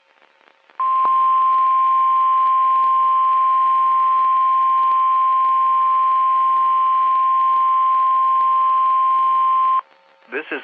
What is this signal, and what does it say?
Emergency Alert System (SAME) tones (the 1050 Hz attention tone), with speech. It is heard from 0.8 to 9.8 s.
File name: eas messages / EAS WXM31 Elmira